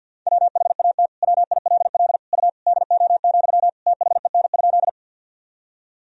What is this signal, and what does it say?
Morse code, heard as a keyed tone, about 47 wpm, 700 Hz. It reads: WHAT WILL U DO, THEN?
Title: Sentences from Top 100 Words and Abbreviations 50wpm